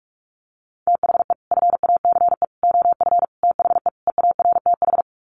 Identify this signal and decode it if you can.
Morse code, heard as a keyed tone, about 44 wpm, 700 Hz. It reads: THE FACE OF THE EARTH
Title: Sets of 5 Words Encore 45wpm